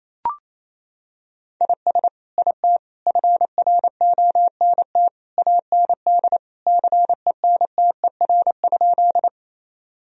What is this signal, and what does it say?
Morse code, heard as a keyed tone, about 28 wpm, 700 Hz. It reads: IS IT FRONT AND CENTER?
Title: Sentences from Top 1000 Words - Review Part 2 - 28wpm